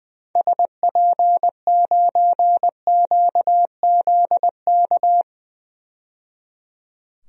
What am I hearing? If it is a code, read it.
Morse code, heard as a keyed tone, about 20 wpm, 700 Hz. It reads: SP9QZK
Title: DXCC-100 Call Signs - 6 Characters - 20wpm